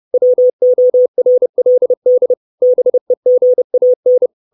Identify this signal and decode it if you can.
Morse code, heard as a keyed tone, about 30 wpm, 500 Hz. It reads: WORLD BEGAN
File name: 632 Words 30WPM